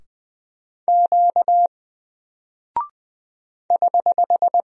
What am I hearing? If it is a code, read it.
Morse code, heard as a keyed tone, about 20 wpm, 700 Hz. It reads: Q <HH>
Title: Single Character (Letter:Number:Punctuation:Prosign) Alt 20wpm